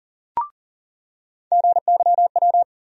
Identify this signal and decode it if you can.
Morse code, heard as a keyed tone, about 40 wpm, 700 Hz. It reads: GYW
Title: USA Call Sign Postfixes - 3 Letters 40wpm